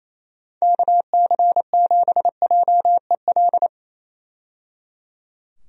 Morse code, heard as a keyed tone, about 27 wpm, 700 Hz. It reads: KC7JEL
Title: USA Call Signs - 6 Characters - 28wpm